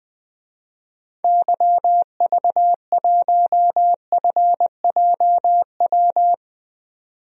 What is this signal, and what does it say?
Morse code, heard as a keyed tone, about 21 wpm, 700 Hz. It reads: YV1FJW
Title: DXCC-100 Call Signs - 6 Characters - 20wpm